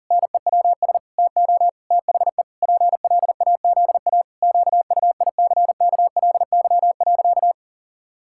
Morse code, heard as a keyed tone, about 40 wpm, 700 Hz. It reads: NEWS TO THE PLAZA QUICKLY.
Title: Pangram Challenge v2 - 40wpm